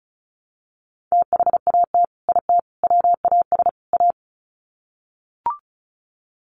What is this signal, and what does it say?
Morse code, heard as a keyed tone, about 34 wpm, 700 Hz. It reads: THAT IT WAS A
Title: Sets of 4 Words 35wpm